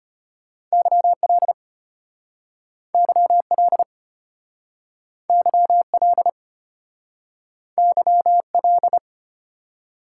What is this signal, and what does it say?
Morse code, heard as a keyed tone, about 33 wpm, 700 Hz. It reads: YL YL YL YL